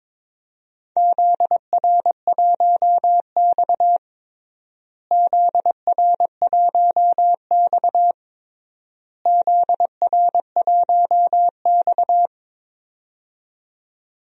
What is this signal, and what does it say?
Morse code, heard as a keyed tone, about 22 wpm, 700 Hz. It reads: ZR1X ZR1X ZR1X
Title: DXCC-100 Call Signs - 4 Characters - Repeated 3x - 22wpm